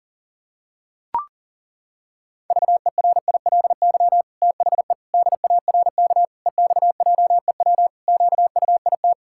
Morse code, heard as a keyed tone, about 40 wpm, 700 Hz. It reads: VERILY THE DARK EXJEW QUIT
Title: Pangram Challenge v2 - 40wpm